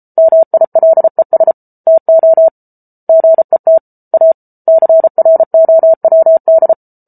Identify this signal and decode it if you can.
Morse code, heard as a keyed tone, about 34 wpm, 650 Hz. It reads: MILES TO GET A CROWD